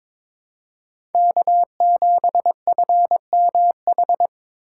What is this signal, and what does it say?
Morse code, heard as a keyed tone, about 22 wpm, 700 Hz. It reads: K7FMH
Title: USA Call Signs - 5 Characters - 3x Character Spacing - 22wpm